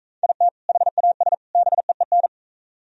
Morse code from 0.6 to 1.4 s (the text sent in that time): HAS